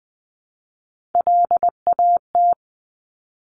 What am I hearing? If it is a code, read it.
Morse code, heard as a keyed tone, about 20 wpm, 700 Hz. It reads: LAT
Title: Top Three Letter Combinations in Words 20wpm